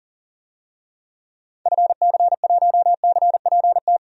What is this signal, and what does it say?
Morse code, heard as a keyed tone, about 37 wpm, 700 Hz. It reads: FC1CPT
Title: DXCC-100 Call Signs - 6 Characters - 40wpm